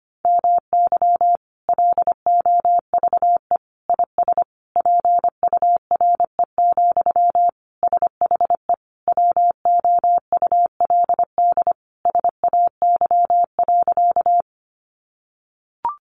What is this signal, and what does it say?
Morse code, heard as a keyed tone, about 25 wpm, 700 Hz. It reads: MY LOVE IS PURE, SHE WOULD SAY.
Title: Sentences from Top 1500 Words 25wpm